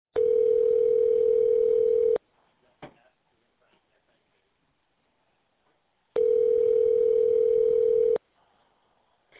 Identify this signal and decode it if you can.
Ringback tone (North America), 440 + 480 Hz.